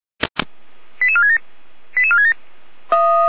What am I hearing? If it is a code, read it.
Five-tone selective call (ZVEI), heard as quick runs of five tones, tones of about 70 ms. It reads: 88467 88467